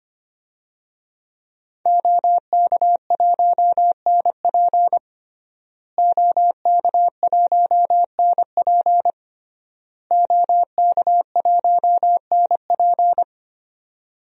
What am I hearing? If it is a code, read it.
Morse code, heard as a keyed tone, about 25 wpm, 700 Hz. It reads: OK1NP OK1NP OK1NP
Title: DXCC-100 Call Signs - 5 Characters - Repeated 3x - 25wpm